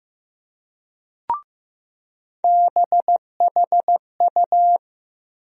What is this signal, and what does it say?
Morse code, heard as a keyed tone, about 15 wpm, 700 Hz. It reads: BHU
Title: USA Call Sign Postfixes - 3 Letters 15wpm